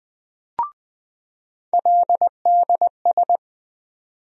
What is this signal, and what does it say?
Morse code, heard as a keyed tone, about 20 wpm, 700 Hz. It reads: LDS
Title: USA Call Sign Postfixes - 3 Letters 20wpm